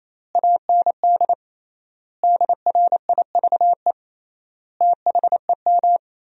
Morse code, heard as a keyed tone, about 28 wpm, 700 Hz. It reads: AND DRIVE THEM
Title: 3x Word-Spacing - Sentences from Top 500 Words - 28wpm